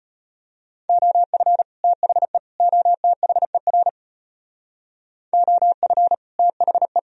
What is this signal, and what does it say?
Morse code, heard as a keyed tone, about 34 wpm, 700 Hz. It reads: OF THE OTHER OF THE